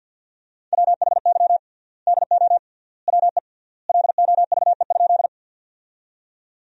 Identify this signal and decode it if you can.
Morse code, heard as a keyed tone, about 51 wpm, 700 Hz. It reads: WHY DO WE LOVE?
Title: 3x Word-Spacing - Sentences from Top 400 Words - 50wpm